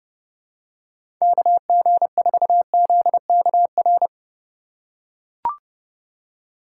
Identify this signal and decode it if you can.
Morse code, heard as a keyed tone, about 30 wpm, 700 Hz. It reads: KG4ZKR
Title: USA Call Signs - 6 Characters - 3x Character Spacing - 30wpm